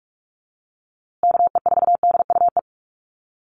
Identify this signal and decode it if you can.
Morse code, heard as a keyed tone, about 44 wpm, 700 Hz. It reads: KE4DUE
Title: USA Call Signs Repeated 2x 45wpm